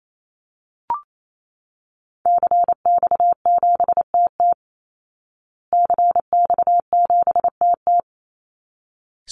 Morse code, heard as a keyed tone, about 28 wpm, 700 Hz. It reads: CX7TT CX7TT
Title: Call Signs Repeated 3x - 28wpm